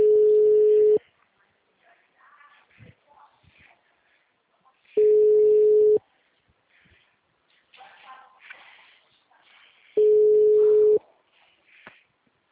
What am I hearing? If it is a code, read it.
Ringback tone (Europe, 425 Hz), with speech.